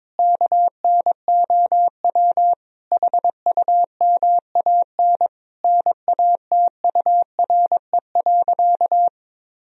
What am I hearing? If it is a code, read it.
Morse code, heard as a keyed tone, about 22 wpm, 700 Hz. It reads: KNOW HUMAN NATURE.
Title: Sentences from Top 1000 Words - Review Part 3 - 22wpm